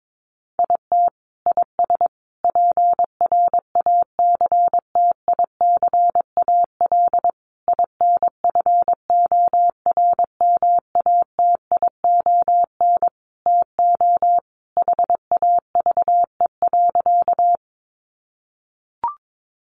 Morse code, heard as a keyed tone, about 22 wpm, 700 Hz. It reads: IT IS PRACTICAL INFORMATION TO HAVE.
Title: Sentences from Top 1400 Words 22wpm